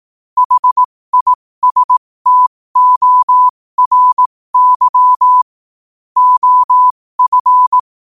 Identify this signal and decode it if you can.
Morse code, heard as a keyed tone, about 18 wpm, spaced as for 15 wpm, 1000 Hz. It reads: HISTORY OF
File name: qotdfiles / 15wpm